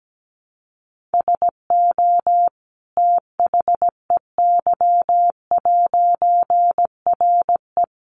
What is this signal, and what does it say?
Morse code, heard as a keyed tone, about 17 wpm, 700 Hz. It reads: SO THEY'RE
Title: A Sound Of Thunder 17wpm